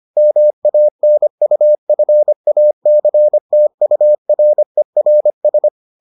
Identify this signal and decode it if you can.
Morse code, heard as a keyed tone, about 25 wpm, 600 Hz. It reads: MANUFACTURERS